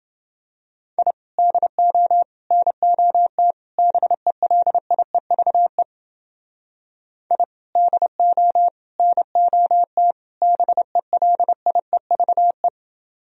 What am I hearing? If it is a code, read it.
Morse code, heard as a keyed tone, about 29 wpm, 700 Hz. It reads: I DO NOT BELIEVE I DO NOT BELIEVE